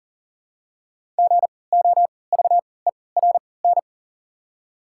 Morse code, heard as a keyed tone, about 40 wpm, spaced as for 23 wpm, 700 Hz. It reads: GOVERN
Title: 3x Character Spacing - Top 500 Words - 40wpm